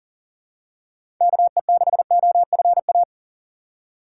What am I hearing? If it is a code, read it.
Morse code, heard as a keyed tone, about 39 wpm, 700 Hz. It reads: KE6OFA
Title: USA Call Signs - 6 Characters - 6x Character Spacing - 40wpm